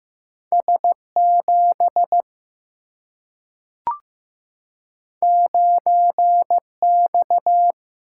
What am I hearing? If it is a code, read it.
Morse code, heard as a keyed tone, about 15 wpm, 700 Hz. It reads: S7 9X
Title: Single-Letter Single-Number 15wpm